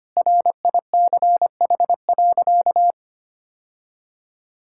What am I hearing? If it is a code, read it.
Morse code, heard as a keyed tone, about 25 wpm, 700 Hz. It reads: RICH.
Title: Sentences from Top 1600 Words 25wpm